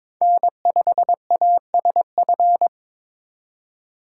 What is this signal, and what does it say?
Morse code, heard as a keyed tone, about 26 wpm, 700 Hz. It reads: N5ASF
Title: Speed Racing - USA 1x3 Call Signs 22wpm